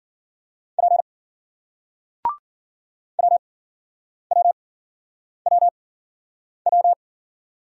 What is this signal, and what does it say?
Morse code, heard as a keyed tone, about 53 wpm, spaced as for 9 wpm, 700 Hz. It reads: P WWWW